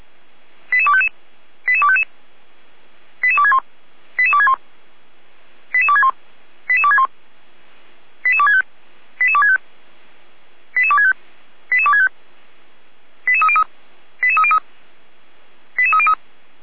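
Five-tone selective call (ZVEI), heard as quick runs of five tones, tones of about 70 ms. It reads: 88255 88255 88261 88261 88261 88261 88265 88265 88265 88265 88303 88303 88303
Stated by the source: SDR-received ZVEI five-tone selective call, address 88247.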